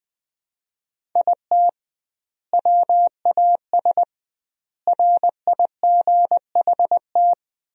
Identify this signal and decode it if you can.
Morse code, heard as a keyed tone, about 21 wpm, 700 Hz. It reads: IT WAS RIGHT
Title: 2x Word-Spacing - Sentences from Top 500 Words - 20wpm